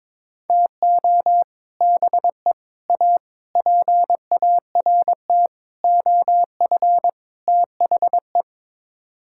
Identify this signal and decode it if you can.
Morse code, heard as a keyed tone, about 22 wpm, 700 Hz. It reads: TO BE A PART OF THE
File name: practice / 22wpm / Sets of 6 Words - 22wpm